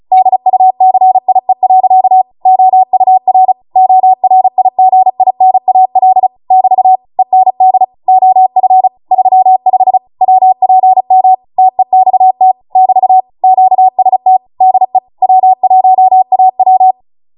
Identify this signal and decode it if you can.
Morse code, heard as a keyed tone, about 36 wpm, 750 Hz. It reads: DUCIE. OUR ORIGINAL <BT> END OF 35 WPM TEXT <BT> QST DE W1AW